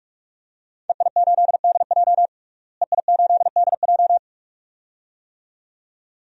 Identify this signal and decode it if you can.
Morse code, heard as a keyed tone, about 41 wpm, 700 Hz. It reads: EI8DJ EI8DJ
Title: DXCC-100 Call Signs - 5 Characters - Repeated 2x - 45wpm